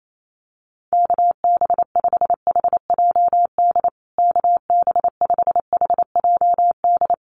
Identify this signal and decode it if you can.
Morse code, heard as a keyed tone, about 28 wpm, 700 Hz. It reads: KB5HJD KB5HJD
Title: USA Call Signs Repeated 2x 28wpm